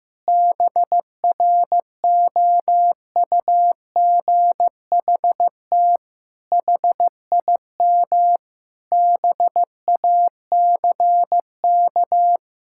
Morse code, heard as a keyed tone, about 15 wpm, 700 Hz. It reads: BROUGHT HIM BACK
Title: Sentences from Top 1000 Words - Review Part 6 - 15wpm